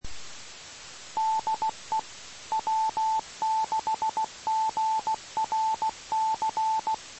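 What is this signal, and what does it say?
Morse code, heard as a keyed tone, about 16 wpm, 880 Hz. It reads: DE W6GRC
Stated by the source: Morse signal recorded on the 2 m band, 147.105 MHz.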